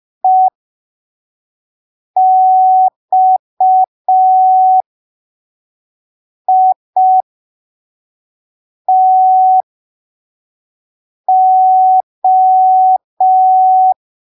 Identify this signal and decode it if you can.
Morse code, heard as a keyed tone, about 5 wpm, spaced as for 3 wpm, 750 Hz. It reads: EXITO
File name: arquivos / texto / texto 09 05ppm